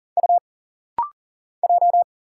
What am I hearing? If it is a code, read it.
Morse code, heard as a keyed tone, about 39 wpm, 700 Hz. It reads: U J
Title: Single Letter-Number - Rapid-Fire 40wpm